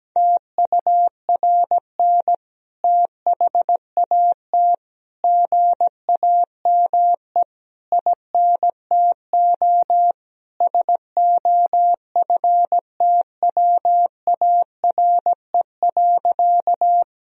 Morse code, heard as a keyed tone, about 17 wpm, 700 Hz. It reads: TURN THAT GAME INTO SOFTWARE.
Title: Sentences from Top 1000 Words - Review Part 3 - 17wpm